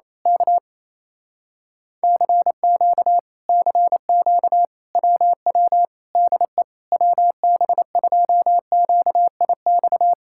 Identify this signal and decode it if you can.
Morse code, heard as a keyed tone, about 27 wpm, 700 Hz. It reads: K CQ CQ WW DE WB2QIX
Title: Speed-Racing Common QSO Phrases - 28wpm